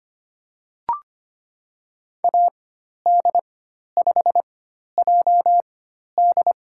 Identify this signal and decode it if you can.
Morse code, heard as a keyed tone, about 25 wpm, spaced as for 12 wpm, 700 Hz. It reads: AD5JD
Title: USA Call Signs - 4x Character Spacing - 25wpm